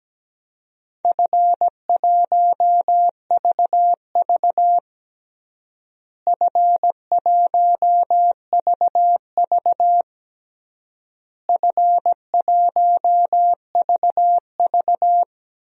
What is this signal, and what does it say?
Morse code, heard as a keyed tone, about 17 wpm, 700 Hz. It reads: F1VV F1VV F1VV
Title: DXCC-100 Call Signs - 4 Characters - Repeated 3x - 17wpm